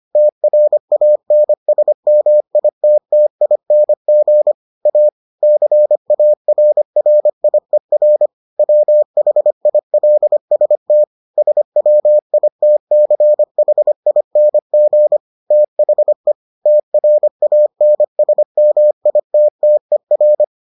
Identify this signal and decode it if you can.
Morse code, heard as a keyed tone, about 25 wpm, 600 Hz. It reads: TRANSMITTING A CARRIER WHILST SWITCHING THE TRANSMITTER